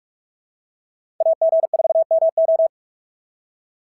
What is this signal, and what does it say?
Morse code, heard as a keyed tone, about 46 wpm, 680 Hz. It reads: AG4MO
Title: SOTA QSO Practice 45wpm